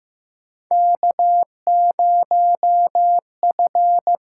Morse code, heard as a keyed tone, about 15 wpm, 700 Hz. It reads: K0F
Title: USA Call Signs - 4 Characters - 6x Character Spacing - 15wpm